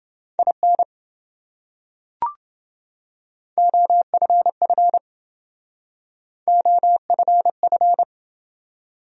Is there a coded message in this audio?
Yes: Morse code from 0.4 to 8.0 s, heard as a keyed tone, about 27 wpm, 700 Hz. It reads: IN OFF OFF